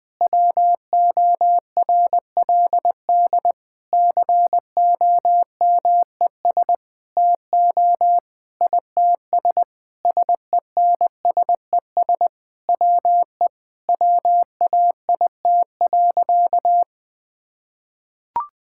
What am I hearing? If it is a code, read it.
Morse code, heard as a keyed tone, about 19 wpm, 700 Hz. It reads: WORLD COMES TO ITS SENSES WE WAIT.
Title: Sentences with 9-Word Phrases - 20wpm